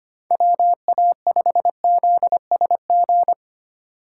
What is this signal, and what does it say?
Morse code, heard as a keyed tone, about 25 wpm, 700 Hz. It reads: WA5ZSG
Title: USA Call Signs - 6 Characters - 3x Character Spacing - 25wpm